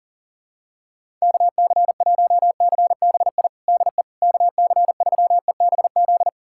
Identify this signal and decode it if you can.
Morse code, heard as a keyed tone, about 37 wpm, 700 Hz. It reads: KC1CBI DE KC3EBZ
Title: USA Callsign DE Callsign - 6 Characters - 40wpm